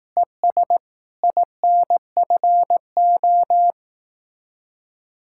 Morse code, heard as a keyed tone, about 18 wpm, 700 Hz. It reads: ES INFO
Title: Speed-Racing Common QSO Phrases Encore - 15wpm